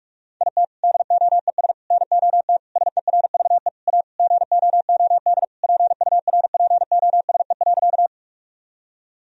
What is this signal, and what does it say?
Morse code, heard as a keyed tone, about 47 wpm, 700 Hz. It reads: IT DOES NOT SERVE A GOOD PURPOSE.
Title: Sentences from Top 1000 Words - Review Part 2 - 45wpm